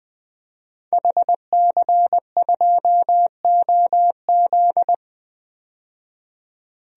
Morse code, heard as a keyed tone, about 20 wpm, 700 Hz. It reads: HC2OZ
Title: DXCC-100 Call Signs - 5 Characters - 20wpm